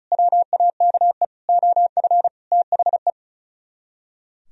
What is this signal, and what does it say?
Morse code, heard as a keyed tone, about 46 wpm, 700 Hz. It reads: WAKE OF THE